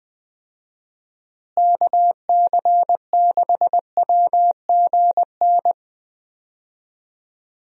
Morse code, heard as a keyed tone, about 20 wpm, 700 Hz. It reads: KC6WGN
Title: USA Call Signs - 6 Characters - Repeated 2x - 20wpm